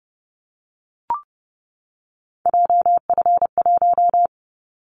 Morse code, heard as a keyed tone, about 30 wpm, 700 Hz. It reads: JF1
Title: Callsign Prefixes for Top 25 DXCC - 30wpm